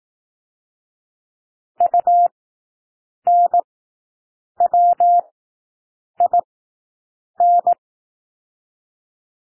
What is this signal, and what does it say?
Morse code, heard as a keyed tone, about 18 wpm, spaced as for 7 wpm, 700 Hz. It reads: UNWIN
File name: SOUND WORDS 18 wpm